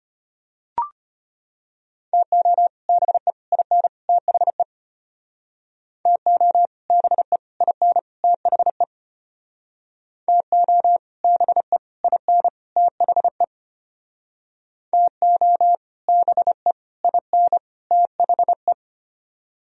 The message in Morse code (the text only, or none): TO BE IN THE TO BE IN THE TO BE IN THE TO BE IN THE